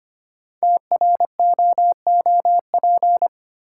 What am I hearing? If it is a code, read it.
Morse code, heard as a keyed tone, about 26 wpm, 700 Hz. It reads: TROOP
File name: 5 Letter Words - First 3 Letters Given - 25wpm